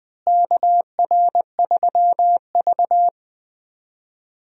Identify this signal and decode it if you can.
Morse code, heard as a keyed tone, about 20 wpm, 700 Hz. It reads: KR3V
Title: USA Call Signs - 4 Characters - Repeated 2x - 20wpm